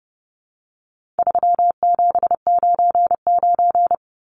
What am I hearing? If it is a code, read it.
Morse code, heard as a keyed tone, about 30 wpm, 700 Hz. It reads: 3799
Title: Numbers - 4-Digit 30wpm